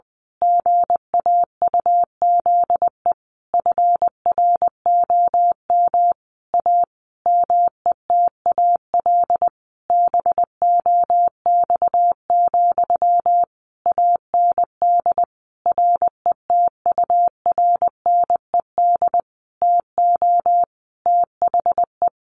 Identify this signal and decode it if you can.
Morse code, heard as a keyed tone, about 20 wpm, 700 Hz. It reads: GAUZE FROM A METAL BOX, AND RETURNED TO THE
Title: A Sound Of Thunder 20wpm